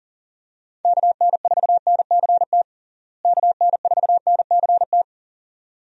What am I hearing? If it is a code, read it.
Morse code, heard as a keyed tone, about 40 wpm, 700 Hz. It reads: KN4NCT KN4NCT
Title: USA Call Signs - 6 Characters - Repeated 2x - 40wpm